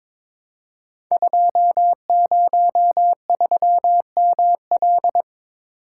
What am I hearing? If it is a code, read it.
Morse code, heard as a keyed tone, about 23 wpm, 700 Hz. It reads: 203ML
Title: Code Groups - 5 Letter:Numbers - 22wpm